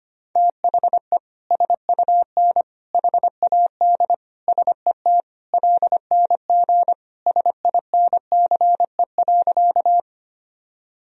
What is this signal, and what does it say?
Morse code, heard as a keyed tone, about 25 wpm, 700 Hz. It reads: THE SUN HAD SET LNG SINCE.
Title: Sentences from Top 300 Words and Abbreviations 25wpm